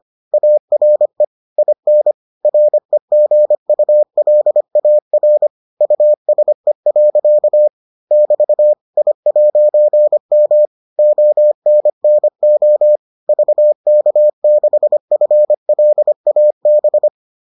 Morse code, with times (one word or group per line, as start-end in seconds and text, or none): 0.3-1.2 ARE
1.6-2.1 IN
2.4-5.5 REGULAR
5.8-7.7 USE.
8.1-8.7 <BT>
9.0-10.7 I'M
11.0-13.0 ONNO
13.3-17.1 VK6FLAB